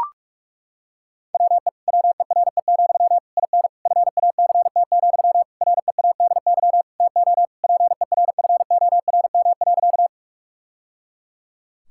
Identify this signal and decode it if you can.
Morse code, heard as a keyed tone, about 41 wpm, 690 Hz. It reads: WE WERE, IN FACT, READY TO PERFORM.
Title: Sentences from Top 1000 Words - Review Part 5 - 45wpm